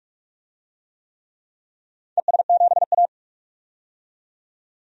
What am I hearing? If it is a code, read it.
Morse code, heard as a keyed tone, about 41 wpm, 700 Hz. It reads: ES7A
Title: DXCC-100 Call Signs - 4 Characters - 45wpm